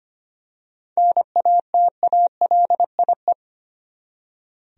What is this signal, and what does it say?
Morse code, heard as a keyed tone, about 25 wpm, 700 Hz. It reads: NATALIE
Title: USA Common Names - 25wpm